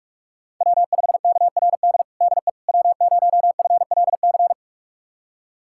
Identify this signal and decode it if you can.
Morse code, heard as a keyed tone, about 46 wpm, 700 Hz. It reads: W5KRD DE W0FLC
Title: USA Callsign DE Callsign - 5 Characters - 45wpm